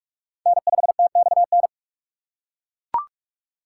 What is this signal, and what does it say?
Morse code, heard as a keyed tone, about 40 wpm, 700 Hz. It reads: N5TXN